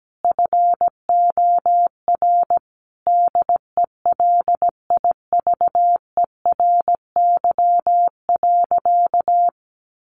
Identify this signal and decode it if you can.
Morse code, heard as a keyed tone, about 17 wpm, 700 Hz. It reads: FOR DELIVERY.
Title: Sentences from Top 1500 Words 17wpm